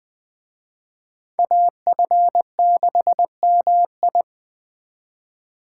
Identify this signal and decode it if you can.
Morse code, heard as a keyed tone, about 20 wpm, 700 Hz. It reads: AF6MI